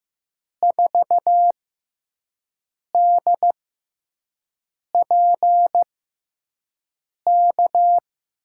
Morse code, heard as a keyed tone, about 15 wpm, spaced as for 5 wpm, 700 Hz. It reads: 4DPK
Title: USA Call Signs - 6 Characters - 6x Character Spacing - 15wpm